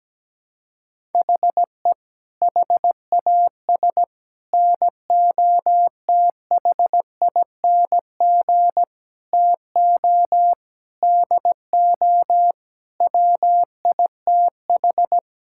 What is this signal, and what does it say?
Morse code, heard as a keyed tone, about 17 wpm, 700 Hz. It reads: HE HAS NOTHING TO DO WITH
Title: Sentences from Top 1000 Words - Review Part 5 - 17wpm